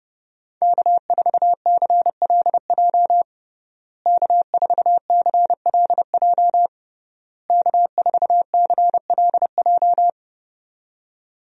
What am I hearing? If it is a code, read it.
Morse code, heard as a keyed tone, about 30 wpm, 700 Hz. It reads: K4CLJ K4CLJ K4CLJ
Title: USA Call Signs - 5 Characters - Repeated 3x - 30wpm